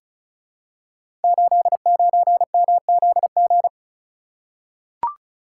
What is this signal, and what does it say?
Morse code, heard as a keyed tone, about 36 wpm, 700 Hz. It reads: 89MZG
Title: Code Groups - 5 Letter:Numbers - 35wpm